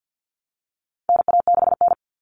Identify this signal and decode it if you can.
Morse code, heard as a keyed tone, about 50 wpm, 700 Hz. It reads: NA6N
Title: Common Cut Numbers - 4-Digit 50wpm